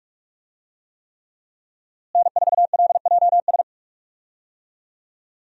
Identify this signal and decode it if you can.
Morse code, heard as a keyed tone, about 46 wpm, 700 Hz. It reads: N4LJS
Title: USA Call Signs - 5 Characters - Repeated 3x - 45wpm